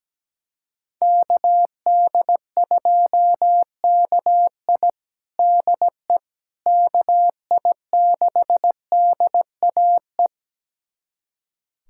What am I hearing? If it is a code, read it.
Morse code, heard as a keyed tone, about 17 wpm, 700 Hz. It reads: KD2KI DE KI6DAE
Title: USA Callsign DE Callsign 17wpm